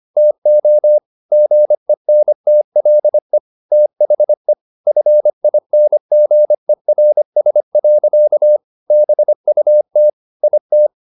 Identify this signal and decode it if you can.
Morse code, heard as a keyed tone, about 25 wpm, 600 Hz. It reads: TO GENTLE THE FINGERS. BUT IT
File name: the old man and the sea part 1 25wpm